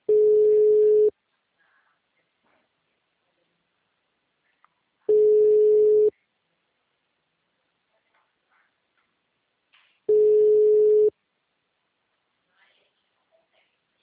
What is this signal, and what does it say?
Ringback tone (Europe, 425 Hz), with speech.
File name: out- -2017- - - .27183